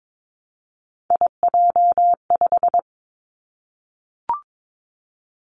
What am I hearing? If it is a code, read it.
Morse code, heard as a keyed tone, about 22 wpm, 700 Hz. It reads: IJ5
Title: Three Challenging Letter-Number Combinations 22wpm